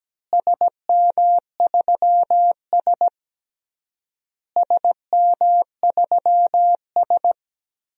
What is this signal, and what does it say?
Morse code, heard as a keyed tone, about 17 wpm, 700 Hz. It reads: SM3S SM3S
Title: DXCC-100 Call Signs - 4 Characters - Repeated 3x - 17wpm